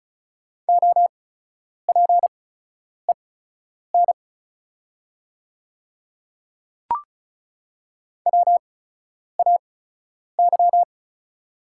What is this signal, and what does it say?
Morse code, heard as a keyed tone, about 34 wpm, spaced as for 10 wpm, 700 Hz. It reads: OPEN WAY
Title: 8x Character Spacing - Top 500 Words - 35wpm